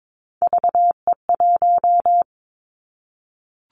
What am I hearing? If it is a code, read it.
Morse code, heard as a keyed tone, about 22 wpm, 700 Hz. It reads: VE1
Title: Callsign Prefixes for Top 100 DXCC - 22wpm